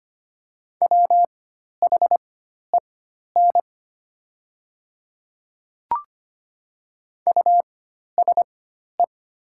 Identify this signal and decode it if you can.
Morse code, heard as a keyed tone, about 25 wpm, spaced as for 12 wpm, 700 Hz. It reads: WHEN USE
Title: 4x Character Spacing - Top 100 Words - 25wpm